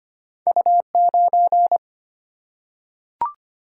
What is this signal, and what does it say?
Morse code, heard as a keyed tone, about 25 wpm, 700 Hz. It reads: U9
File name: Two Letter-Number Combinations - 25wpm